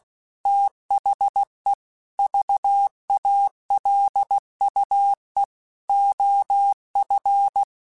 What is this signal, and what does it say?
Morse code, heard as a keyed tone, about 16 wpm, 800 Hz. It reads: THE VALUE OF
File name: Technical Subjects / Technical Subjects-mp3s / XC @15